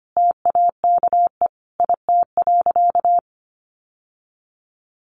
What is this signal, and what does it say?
Morse code, heard as a keyed tone, about 25 wpm, 700 Hz. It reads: TAKE IT.
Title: Sentences from Top 700 Words 25wpm